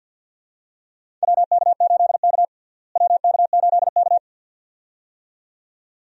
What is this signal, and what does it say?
Morse code, heard as a keyed tone, about 50 wpm, 700 Hz. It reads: WK8K WK8K
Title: USA Call Signs - 4 Characters - Repeated 2x - 50wpm